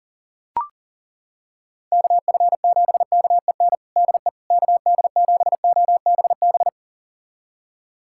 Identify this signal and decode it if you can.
Morse code, heard as a keyed tone, about 37 wpm, 700 Hz. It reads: KF7KEN DE KD7OBB
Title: USA Callsign DE Callsign - 6 Characters - 40wpm